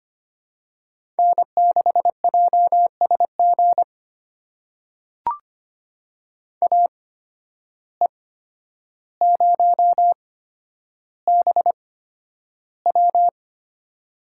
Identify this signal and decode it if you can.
Morse code, heard as a keyed tone, about 25 wpm, spaced as for 11 wpm, 700 Hz. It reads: N6JSG AE0BW
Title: USA Call Signs - 5 Characters - 8x Character Spacing - 25wpm